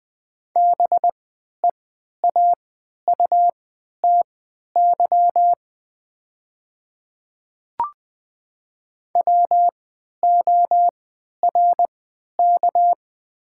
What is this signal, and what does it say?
Morse code, heard as a keyed tone, about 20 wpm, spaced as for 11 wpm, 700 Hz. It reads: BEAUTY WORK
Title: 3x Character Spacing - Top 500 Words - 20wpm